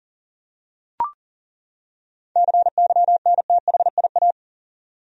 Morse code, heard as a keyed tone, about 40 wpm, 700 Hz. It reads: CYNTHIA
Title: USA Common Names - 40wpm